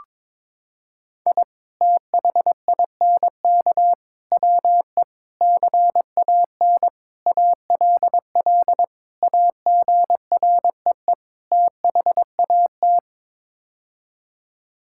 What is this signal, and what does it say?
Morse code, heard as a keyed tone, about 22 wpm, 700 Hz. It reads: I THINK WE CAN ALL AGREE THAT
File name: practice / 22wpm / Sets of 7 Words - 22wpm